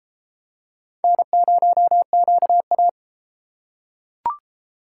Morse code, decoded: N0QA

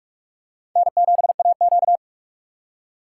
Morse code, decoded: N7AQ

